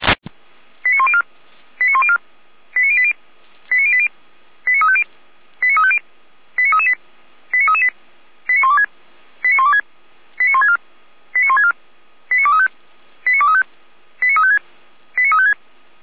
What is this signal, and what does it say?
Five-tone selective call (ZVEI), heard as quick runs of five tones, tones of about 70 ms. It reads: 89104 89104 89080 89080 89350 89350 89338 89338 89126 89126 89164 89164 89235 89235 89356 89356